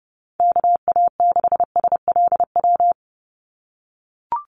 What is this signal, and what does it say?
Morse code, heard as a keyed tone, about 30 wpm, 700 Hz. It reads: KA6SLW
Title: USA Call Signs 30wpm